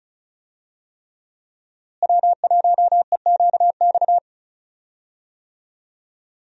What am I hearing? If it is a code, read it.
Morse code, heard as a keyed tone, about 36 wpm, 700 Hz. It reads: W1EQX
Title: USA Call Signs - 5 Characters - Repeated 3x - 35wpm